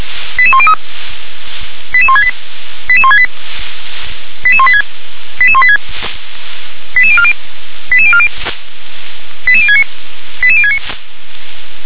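Five-tone selective call (ZVEI), heard as quick runs of five tones, tones of about 70 ms. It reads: 88103 88168 88168 88186 88186 88040 88040 88069 88069
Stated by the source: Received by SDR, ZVEI five-tone selective call.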